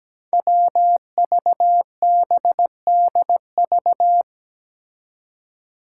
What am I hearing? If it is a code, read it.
Morse code, heard as a keyed tone, about 17 wpm, 700 Hz. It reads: WVBDV
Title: Five Challenging Letter-Number Combinations 17wpm